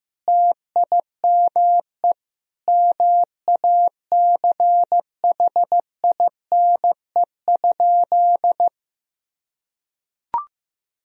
Morse code, heard as a keyed tone, about 15 wpm, 700 Hz. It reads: TIME MACHINE?
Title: Sentences from Top 1000 Words - Review Part 2 - 15wpm